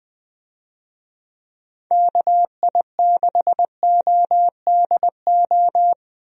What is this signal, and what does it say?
Morse code, heard as a keyed tone, about 20 wpm, 700 Hz. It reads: KI6ODO
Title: USA Call Signs - 6 Characters - Repeated 3x - 20wpm